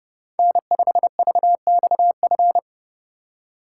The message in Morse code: N5VXF

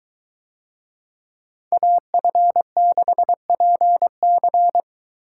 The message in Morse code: AF6PC

